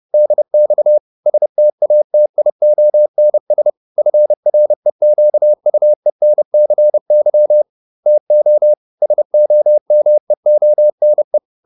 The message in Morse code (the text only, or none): DX STATIONS FREQUENCY TO SOMEONE